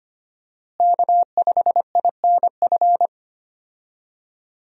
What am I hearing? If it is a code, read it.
Morse code, heard as a keyed tone, about 25 wpm, 700 Hz. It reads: K5INF
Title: USA Call Signs - 5 Characters - Repeated 3x - 25wpm